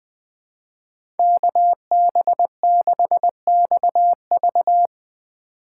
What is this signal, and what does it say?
Morse code, heard as a keyed tone, about 20 wpm, 700 Hz. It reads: KB6XV
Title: USA Call Signs - 2x Character Spacing - 20wpm